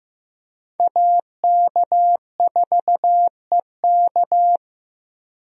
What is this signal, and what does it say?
Morse code, heard as a keyed tone, about 15 wpm, 700 Hz. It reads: AK4EK
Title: USA Call Signs - 5 Characters - 15wpm